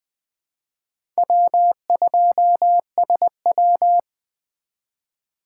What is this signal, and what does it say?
Morse code, heard as a keyed tone, about 20 wpm, 700 Hz. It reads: W2SW
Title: USA Call Signs - 4 Characters - 3x Character Spacing - 20wpm